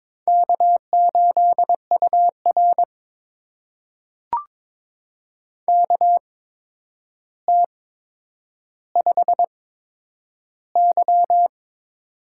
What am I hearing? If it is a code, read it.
Morse code, heard as a keyed tone, about 22 wpm, spaced as for 9 wpm, 700 Hz. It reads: K8UR KT5Y
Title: USA Call Signs - 4 Characters - 8x Character Spacing - 22wpm